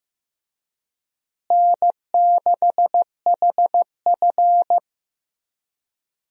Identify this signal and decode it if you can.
Morse code, heard as a keyed tone, about 15 wpm, 700 Hz. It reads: N6HF
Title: USA Call Signs - 4 Characters - Repeated 2x - 15wpm